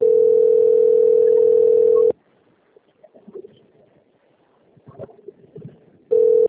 Ringback tone (North America), 440 + 480 Hz.